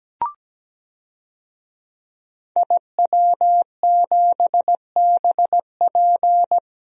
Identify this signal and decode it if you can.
Morse code, heard as a keyed tone, about 17 wpm, 700 Hz. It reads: IW7BP
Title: DXCC-100 Call Signs - 5 Characters - Repeated 3x - 17wpm